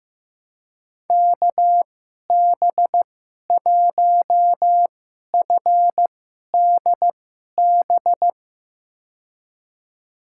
Morse code, heard as a keyed tone, about 15 wpm, spaced as for 11 wpm, 700 Hz. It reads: KB1FDB
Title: USA Call Signs - 2x Character Spacing - 15wpm